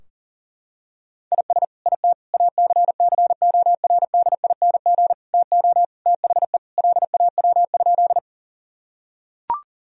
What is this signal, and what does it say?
Morse code, heard as a keyed tone, about 37 wpm, 700 Hz. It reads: IS IT ACCORDING TO THE LAW?
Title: Sentences from Top 1000 Words - Review Part 4 - 40wpm